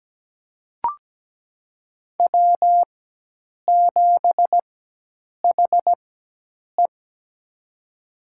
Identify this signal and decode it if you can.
Morse code, heard as a keyed tone, about 17 wpm, spaced as for 8 wpm, 700 Hz. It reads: W7HE
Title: USA Call Signs - 4 Characters - 4x Character Spacing - 17wpm